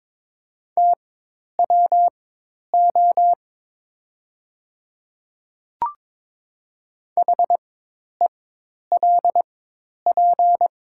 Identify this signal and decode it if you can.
Morse code, heard as a keyed tone, about 22 wpm, spaced as for 10 wpm, 700 Hz. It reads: TWO HELP
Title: 4x Character Spacing - Top 200 Words - 22wpm